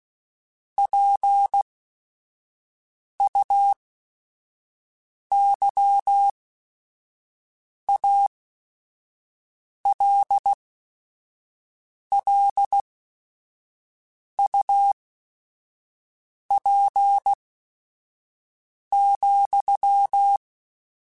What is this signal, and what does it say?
Morse code, heard as a keyed tone, about 16 wpm, spaced as for 5 wpm, 800 Hz. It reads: PUYALLUP,